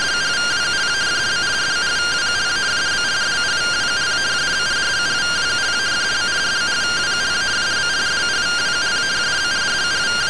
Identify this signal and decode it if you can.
Two-tone teleprinter signal, about 170 Hz shift.